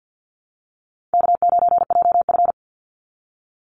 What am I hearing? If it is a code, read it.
Morse code, heard as a keyed tone, about 50 wpm, 700 Hz. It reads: K9JF
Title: Easy Call Signs Repeated 2x - 50wpm